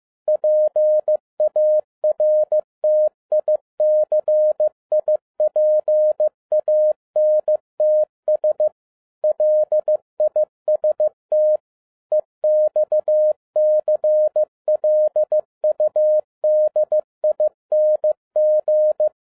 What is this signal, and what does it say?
Morse code, heard as a keyed tone, about 15 wpm, 610 Hz. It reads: PARTICIPANTS LIST EXCLUDING
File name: 2012-01-mp3 / n17-SUMMARY OF FEA NET IN 2011-15wpm